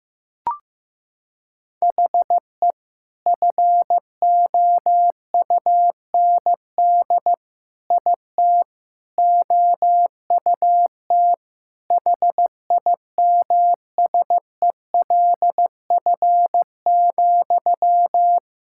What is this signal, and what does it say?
Morse code, heard as a keyed tone, about 15 wpm, 700 Hz. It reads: HE FOUND IT OUT HIMSELF,
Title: Sentences from Top 1600 Words 15wpm